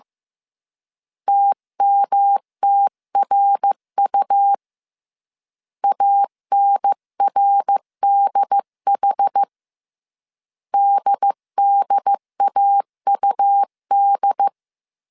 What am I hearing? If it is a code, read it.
Morse code, heard as a keyed tone, about 15 wpm, 800 Hz. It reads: TMTRU ANRDH DDAUD
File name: CW / DET / 1 3 1500C S C 14wpm 2458